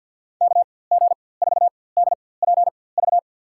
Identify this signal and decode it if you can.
Morse code, heard as a keyed tone, about 49 wpm, spaced as for 23 wpm, 700 Hz. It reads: KG4DPV
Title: USA Call Signs - 6 Characters - 4x Character Spacing - 50wpm